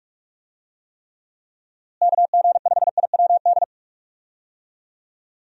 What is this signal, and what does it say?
Morse code, heard as a keyed tone, about 46 wpm, 700 Hz. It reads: KG5IWD
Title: USA Call Signs - 6 Characters - Repeated 2x - 45wpm